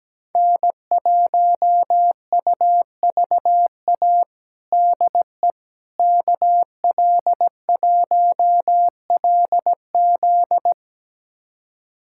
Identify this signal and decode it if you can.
Morse code, heard as a keyed tone, about 17 wpm, 700 Hz. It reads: N1UVA DE KL1LZ
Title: USA Callsign DE Callsign - 5 Characters - 17wpm